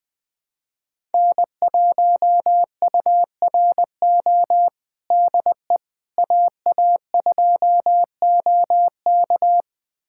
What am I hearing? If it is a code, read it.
Morse code, heard as a keyed tone, about 20 wpm, 700 Hz. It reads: N1URO DE AA2OK
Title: USA Callsign DE Callsign - 5 Characters - 20wpm